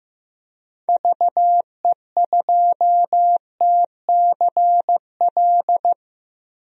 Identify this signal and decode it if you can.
Morse code, heard as a keyed tone, about 15 wpm, 700 Hz. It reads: VE2TCL
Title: DXCC-100 Call Signs - 6 Characters - 15wpm